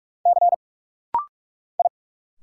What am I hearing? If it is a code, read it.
Morse code, heard as a keyed tone, about 44 wpm, 700 Hz. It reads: C I
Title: Single Letter-Number Alt - Rapid-Fire 45wpm